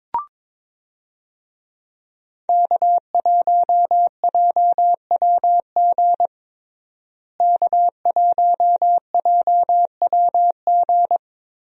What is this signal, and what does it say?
Morse code, heard as a keyed tone, about 22 wpm, 700 Hz. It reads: K1JWG K1JWG
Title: USA Call Signs - 5 Characters - Repeated 3x - 22wpm